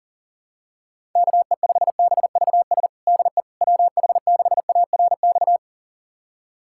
Morse code, heard as a keyed tone, about 37 wpm, 700 Hz. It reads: KE5BVS DE WH6ARX
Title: USA Callsign DE Callsign - 6 Characters - 40wpm